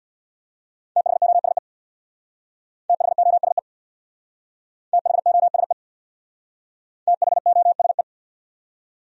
Morse code, heard as a keyed tone, about 60 wpm, 700 Hz. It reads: THOSE THOSE THOSE THOSE
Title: Speed-Racing Top 500 Words 50wpm